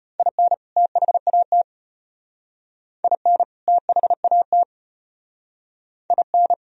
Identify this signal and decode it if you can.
Morse code, heard as a keyed tone, about 33 wpm, 700 Hz. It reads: IN THAT IN THAT IN